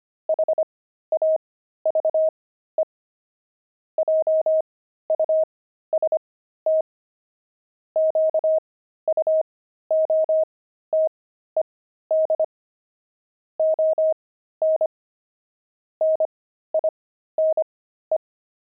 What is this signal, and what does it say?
Morse code, heard as a keyed tone, about 25 wpm, spaced as for 13 wpm, 630 Hz. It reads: HAVE JUST QUOTED ON NINE
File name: pangram-fs-13-cs-25-nine